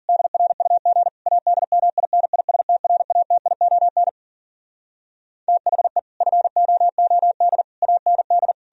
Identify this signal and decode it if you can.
Morse code, heard as a keyed tone, about 56 wpm, 700 Hz. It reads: DRUG ADMINISTRATION THE FOOD AND